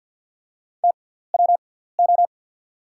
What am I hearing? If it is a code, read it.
Morse code, heard as a keyed tone, about 49 wpm, spaced as for 17 wpm, 700 Hz. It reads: TWO
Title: 6x Character Spacing - Top 200 Words - 50wpm